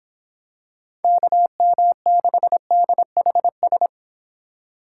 Morse code, heard as a keyed tone, about 26 wpm, 700 Hz. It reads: KM6DHS